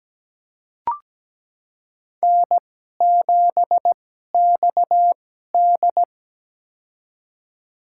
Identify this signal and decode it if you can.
Morse code, heard as a keyed tone, about 17 wpm, spaced as for 12 wpm, 700 Hz. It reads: N7XD
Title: USA Call Signs - 4 Characters - 2x Character Spacing - 17wpm